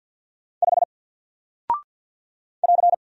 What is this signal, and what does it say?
Morse code, heard as a keyed tone, about 49 wpm, 700 Hz. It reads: 5 <AR>
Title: Single Character (Letter:Number:Punctuation:Prosign) Alt 50wpm